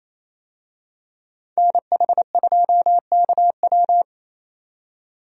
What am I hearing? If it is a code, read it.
Morse code, heard as a keyed tone, about 28 wpm, 700 Hz. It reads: NH2KW
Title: USA Call Signs - 5 Characters - 28wpm